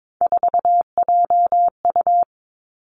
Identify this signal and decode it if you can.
Morse code, heard as a keyed tone, about 22 wpm, 700 Hz. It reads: 4JU
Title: Three Challenging Letter-Number Combinations 22wpm